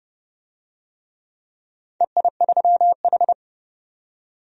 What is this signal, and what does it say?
Morse code, heard as a keyed tone, about 31 wpm, 700 Hz. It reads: EI3H
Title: DXCC-100 Call Signs - 4 Characters - 30wpm